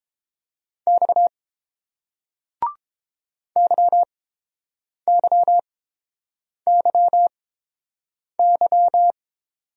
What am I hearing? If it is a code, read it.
Morse code, heard as a keyed tone, about 30 wpm, spaced as for 7 wpm, 690 Hz. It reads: X YYYY